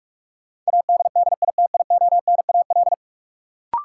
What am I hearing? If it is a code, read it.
Morse code, heard as a keyed tone, about 45 wpm, 700 Hz. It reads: ADDITIONAL
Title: Sentences from Top 1400 Words - New Word Drill Alt - 45wpm